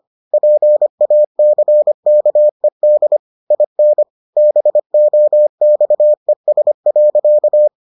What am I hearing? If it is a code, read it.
Morse code, heard as a keyed tone, about 25 wpm, 600 Hz. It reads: PACKED IN BOXES.